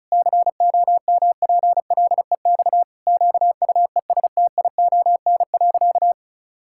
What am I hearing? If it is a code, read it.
Morse code, heard as a keyed tone, about 36 wpm, 700 Hz. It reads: COMPLEX QUESTION.
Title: Sentences from Top 1000 Words - Review Part 6 - 35wpm